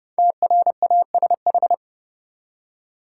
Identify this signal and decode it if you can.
Morse code, heard as a keyed tone, about 30 wpm, 700 Hz. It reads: TRASH
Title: Five Letter Words - 30wpm